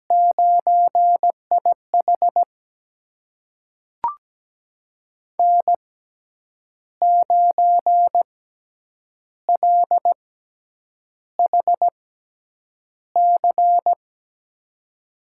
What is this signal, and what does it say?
Morse code, heard as a keyed tone, about 17 wpm, spaced as for 8 wpm, 700 Hz. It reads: 9IH N9LHC
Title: USA Call Signs - 6x Character Spacing - 17wpm